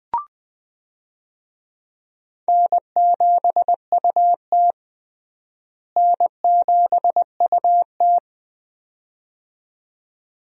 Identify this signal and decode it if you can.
Morse code, heard as a keyed tone, about 20 wpm, 700 Hz. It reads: N7UT N7UT
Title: USA Call Signs - 4 Characters - Repeated 2x - 20wpm